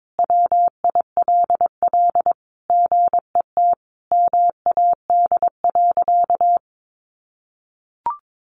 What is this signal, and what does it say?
Morse code, heard as a keyed tone, about 22 wpm, 700 Hz. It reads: WILL GET MAD.
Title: Additional Easy Phrases 22wpm